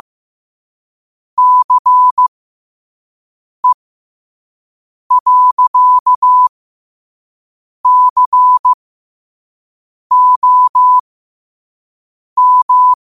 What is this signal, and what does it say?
Morse code, heard as a keyed tone, about 15 wpm, spaced as for 5 wpm, 1000 Hz. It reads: CE.COM
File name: qotdfiles / 5wpm